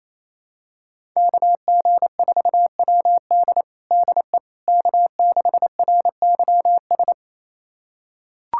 Morse code, heard as a keyed tone, about 28 wpm, 700 Hz. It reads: KG4WD DE K6RYS
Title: USA Callsign DE Callsign - 5 Characters - 28wpm